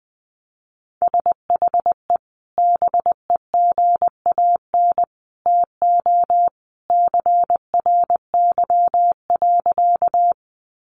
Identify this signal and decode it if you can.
Morse code, heard as a keyed tone, about 20 wpm, 700 Hz. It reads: SHE BEGAN TO CRY.
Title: Sentences from Top 500 Words 20wpm